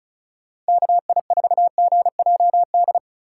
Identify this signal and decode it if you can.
Morse code, heard as a keyed tone, about 36 wpm, 700 Hz. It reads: KI4GJD
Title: USA Call Signs - 6 Characters - 2x Character Spacing - 35wpm